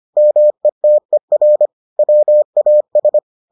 Morse code, heard as a keyed tone, about 25 wpm, 600 Hz. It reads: METER WAS